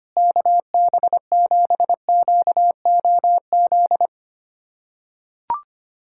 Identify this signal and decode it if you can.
Morse code, heard as a keyed tone, about 25 wpm, 700 Hz. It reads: KB7QOZ
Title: USA Callsign DE Callsign 25wpm